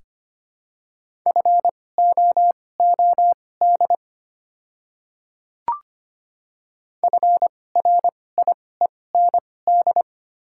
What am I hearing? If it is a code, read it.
Morse code, heard as a keyed tone, about 25 wpm, spaced as for 18 wpm, 700 Hz. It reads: FOOD FRIEND
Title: 2x Character Spacing - Top 300 Words - 25wpm